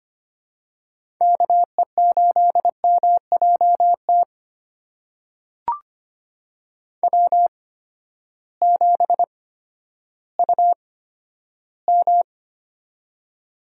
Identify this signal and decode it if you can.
Morse code, heard as a keyed tone, about 25 wpm, spaced as for 12 wpm, 700 Hz. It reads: KE8MJT W7UM
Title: USA Call Signs - 8x Character Spacing - 25wpm